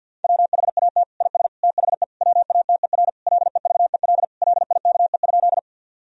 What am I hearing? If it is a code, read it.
Morse code, heard as a keyed tone, about 50 wpm, 700 Hz. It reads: WHAT IS THE WATER LEVEL LIKE?
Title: Sentences from Top 1000 Words - Review Part 4 - 50wpm